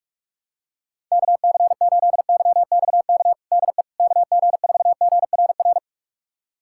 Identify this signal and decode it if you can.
Morse code, heard as a keyed tone, about 46 wpm, 700 Hz. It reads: KC8YXK DE KG4GRR
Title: USA Callsign DE Callsign - 6 Characters - 45wpm